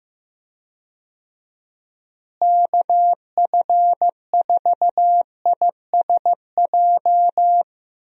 Morse code, heard as a keyed tone, about 15 wpm, 700 Hz. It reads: KF4ISJ
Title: USA Call Signs - 6 Characters - Repeated 3x - 15wpm